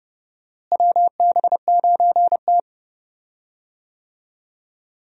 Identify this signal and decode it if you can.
Morse code, heard as a keyed tone, about 30 wpm, 700 Hz. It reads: WB9T
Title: USA Call Signs - 4 Characters - Repeated 2x - 30wpm